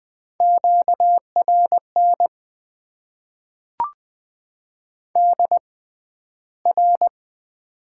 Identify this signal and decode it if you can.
Morse code, heard as a keyed tone, about 20 wpm, spaced as for 12 wpm, 700 Hz. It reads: QRN DR
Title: Top 90 QSO Elements - 6x Character Spacing - 20wpm